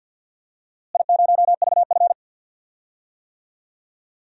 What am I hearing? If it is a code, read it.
Morse code, heard as a keyed tone, about 51 wpm, 700 Hz. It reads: I0VF
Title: DXCC-100 Call Signs - 4 Characters - Repeated 3x - 50wpm